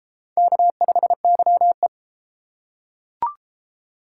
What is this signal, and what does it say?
Morse code, heard as a keyed tone, about 30 wpm, 700 Hz. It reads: K5YE